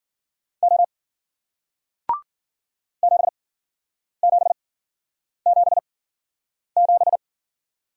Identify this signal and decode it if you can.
Morse code, heard as a keyed tone, about 48 wpm, spaced as for 9 wpm, 700 Hz. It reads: O 7777
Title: Speed-Racing Single Letter-Number 40wpm